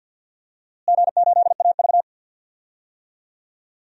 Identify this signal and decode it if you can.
Morse code, heard as a keyed tone, about 46 wpm, 700 Hz. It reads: G8AV
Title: DXCC-100 Call Signs - 4 Characters - Repeated 2x - 50wpm